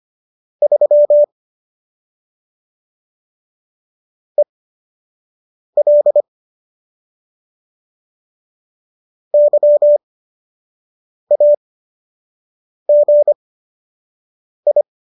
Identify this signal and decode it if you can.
Morse code, heard as a keyed tone, about 25 wpm, spaced as for 6 wpm, 590 Hz. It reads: 3 EL YAGI
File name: qso012 06